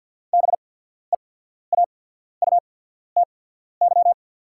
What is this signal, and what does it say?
Morse code, heard as a keyed tone, about 48 wpm, spaced as for 14 wpm, 690 Hz. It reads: BEAUTY
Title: 8x Character Spacing - Top 500 Words - 50wpm